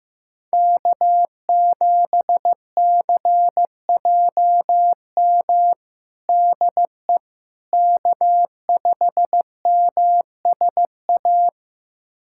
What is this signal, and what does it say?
Morse code, heard as a keyed tone, about 15 wpm, 700 Hz. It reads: K7CJM DE K5MSA
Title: USA Callsign DE Callsign - 5 Characters - 15wpm